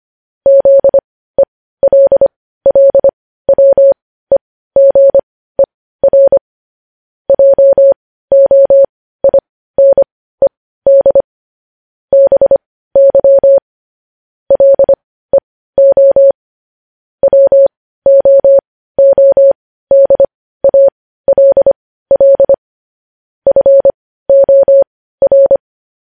Morse code, heard as a keyed tone, about 25 wpm, spaced as for 15 wpm, 550 Hz. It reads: ZELLWEGER JOINED BY LEO WOODALL FOR